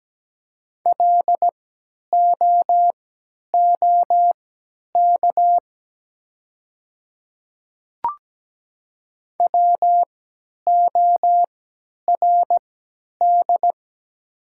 Morse code, heard as a keyed tone, about 17 wpm, spaced as for 10 wpm, 700 Hz. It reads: LOOK WORD
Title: 3x Character Spacing - Top 100 Words - 17wpm